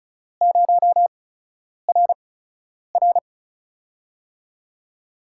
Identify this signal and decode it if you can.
Morse code, heard as a keyed tone, about 34 wpm, spaced as for 10 wpm, 700 Hz. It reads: 0RR
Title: USA Call Signs - 5 Characters - 8x Character Spacing - 35wpm